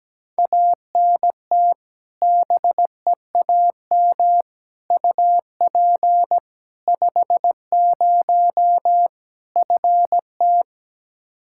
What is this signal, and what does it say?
Morse code, heard as a keyed tone, about 17 wpm, 700 Hz. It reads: ANT BEAM UP 50 FT
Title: Practice QSO - Exchange 2 - Info - 17wpm